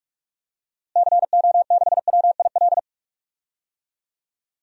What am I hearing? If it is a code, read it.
Morse code, heard as a keyed tone, about 46 wpm, 700 Hz. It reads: CO6WIL
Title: DXCC-100 Call Signs - 6 Characters - Repeated 2x - 45wpm